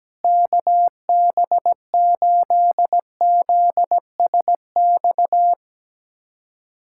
Morse code, heard as a keyed tone, about 17 wpm, 700 Hz. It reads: KB8ZSX
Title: USA Callsign DE Callsign - 6 Characters - 17wpm